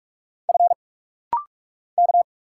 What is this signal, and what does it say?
Morse code, heard as a keyed tone, about 44 wpm, 700 Hz. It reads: F K
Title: Single Letter-Number Alt - Rapid-Fire 45wpm